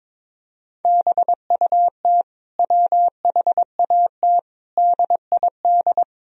Morse code, heard as a keyed tone, about 22 wpm, 700 Hz. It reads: BUT WHAT DID
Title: Sentences from Top 100 Words and Abbreviations 22wpm